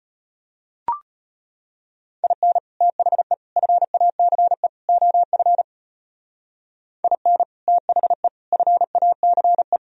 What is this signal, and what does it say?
Morse code, heard as a keyed tone, about 32 wpm, 700 Hz. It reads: IN THE FACE OF IN THE FACE